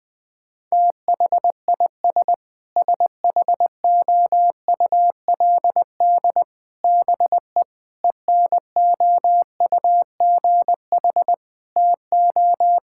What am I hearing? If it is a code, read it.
Morse code, heard as a keyed tone, about 19 wpm, 700 Hz. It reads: THIS SHOULD BE ENOUGH TO
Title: Sentences from Top 1000 Words - Review Part 6 - 20wpm